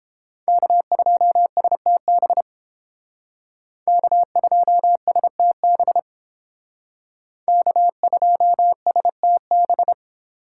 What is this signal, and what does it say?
Morse code, heard as a keyed tone, about 31 wpm, 700 Hz. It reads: K2STB K2STB K2STB